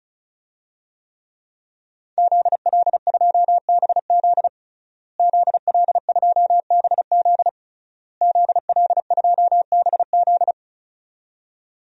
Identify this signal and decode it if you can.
Morse code, heard as a keyed tone, about 34 wpm, 700 Hz. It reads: ZL2BZ ZL2BZ ZL2BZ
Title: DXCC-100 Call Signs - 5 Characters - Repeated 3x - 35wpm